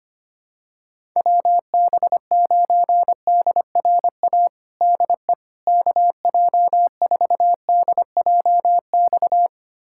Morse code, heard as a keyed tone, about 25 wpm, 700 Hz. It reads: WB9DRA DE KJ4DJX
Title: USA Callsign DE Callsign - 6 Characters - 25wpm